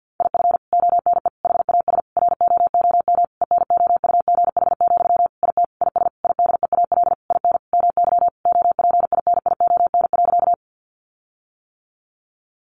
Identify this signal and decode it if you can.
Morse code, heard as a keyed tone, about 48 wpm, 700 Hz. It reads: IF ONE HAS ROOM ENOUGH, IT IS IDEAL IN MY OPINION.
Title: Sentences from Top 1400 Words 50wpm